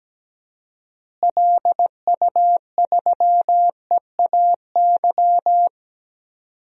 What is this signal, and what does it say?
Morse code, heard as a keyed tone, about 17 wpm, 700 Hz. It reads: LU3EAY
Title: DXCC-100 Call Signs - 6 Characters - Repeated 3x - 17wpm